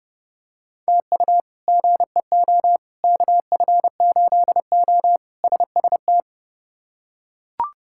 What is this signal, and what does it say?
Morse code, heard as a keyed tone, about 32 wpm, 700 Hz. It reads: TU GEO KF8O SST
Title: K1USN SST QSO Practice - 30wpm